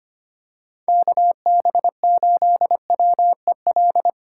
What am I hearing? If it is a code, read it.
Morse code, heard as a keyed tone, about 25 wpm, 700 Hz. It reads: KB8WEL
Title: USA Call Signs - 6 Characters - 3x Character Spacing - 25wpm